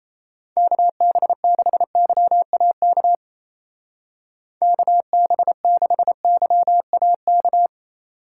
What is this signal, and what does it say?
Morse code, heard as a keyed tone, about 35 wpm, 700 Hz. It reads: KB6YAK KB6YAK
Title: Speed Racing - USA Call Signs 28wpm